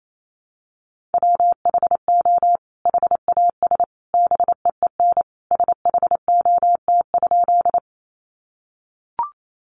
Morse code, heard as a keyed tone, about 28 wpm, 700 Hz. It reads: WHO HAS BEEN SHOT?
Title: Sentences from Top 1100 Words 28wpm